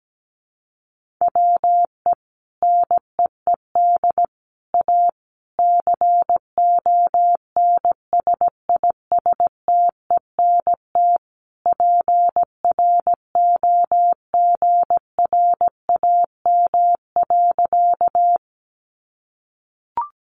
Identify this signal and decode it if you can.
Morse code, heard as a keyed tone, about 17 wpm, 700 Hz. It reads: WE NEED A CONSISTENT PROGRAM.
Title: Sentences from Top 1500 Words 17wpm